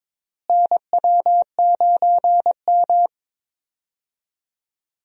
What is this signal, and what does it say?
Morse code, heard as a keyed tone, about 22 wpm, 700 Hz. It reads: NW9M
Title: USA Call Signs - 4 Characters - Repeated 3x - 22wpm